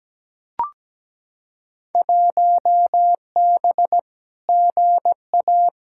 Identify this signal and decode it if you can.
Morse code, heard as a keyed tone, about 17 wpm, 700 Hz. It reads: 1B GA
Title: Field Day Exchanges - 17wpm